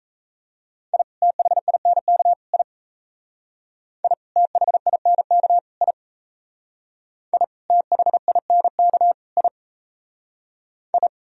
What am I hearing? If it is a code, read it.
Morse code, heard as a keyed tone, about 37 wpm, 700 Hz. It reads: I THINK I I THINK I I THINK I I